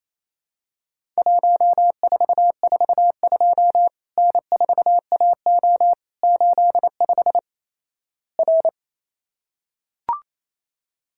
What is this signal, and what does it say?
Morse code, heard as a keyed tone, about 28 wpm, 700 Hz. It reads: 1442 N4AO 85 R
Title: Worked All Europe DX Contest - QTC Exchange - 28wpm